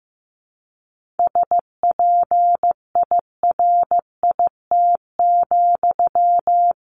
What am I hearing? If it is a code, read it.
Morse code, heard as a keyed tone, about 15 wpm, 700 Hz. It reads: SPIRIT,
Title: Sentences from Top 1400 Words 15wpm